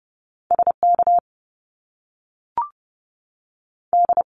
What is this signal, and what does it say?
Morse code, heard as a keyed tone, about 30 wpm, 700 Hz. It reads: SK D
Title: Top Two Letter Combinations in Words 30wpm